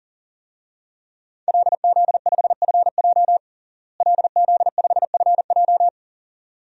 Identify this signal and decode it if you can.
Morse code, heard as a keyed tone, about 40 wpm, 690 Hz. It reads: LZ5FJ LZ5FJ
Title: DXCC-100 Call Signs - 5 Characters - Repeated 2x - 40wpm